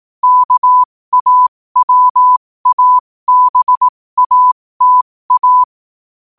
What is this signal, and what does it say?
Morse code, heard as a keyed tone, about 18 wpm, spaced as for 15 wpm, 1000 Hz. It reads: KAWABATA
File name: qotdfiles / 15wpm